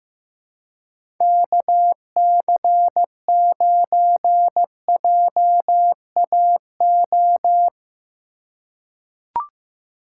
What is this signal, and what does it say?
Morse code, heard as a keyed tone, about 15 wpm, 700 Hz. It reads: KC9JAO
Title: USA Call Signs - 6 Characters - 4x Character Spacing - 15wpm